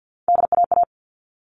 Morse code, heard as a keyed tone, about 50 wpm, 700 Hz. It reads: DAA
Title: All Cut Numbers - 3-Digit 50wpm